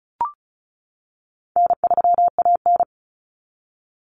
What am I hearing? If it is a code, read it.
Morse code, heard as a keyed tone, about 34 wpm, 700 Hz. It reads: N3AN
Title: Common Cut Numbers - 4-Digit 35wpm